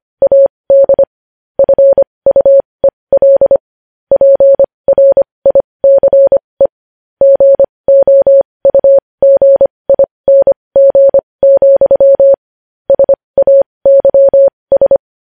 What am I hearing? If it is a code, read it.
Morse code, heard as a keyed tone, about 25 wpm, spaced as for 20 wpm, 550 Hz. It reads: AD FUEL PRICE GOUGING, SAYS
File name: podcast / News headlines-1x-25x20